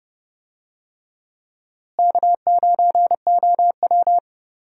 Morse code, heard as a keyed tone, about 30 wpm, 700 Hz. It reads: K9OW
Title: USA Call Signs - 4 Characters - Repeated 3x - 30wpm